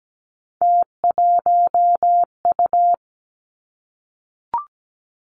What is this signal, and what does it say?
Morse code, heard as a keyed tone, about 17 wpm, 700 Hz. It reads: T1U
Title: Three Challenging Letter-Number Combinations 17wpm